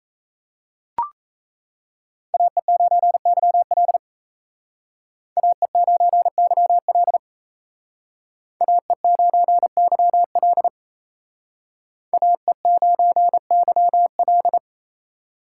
Morse code, heard as a keyed tone, about 35 wpm, 700 Hz. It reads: AE9YL AE9YL AE9YL AE9YL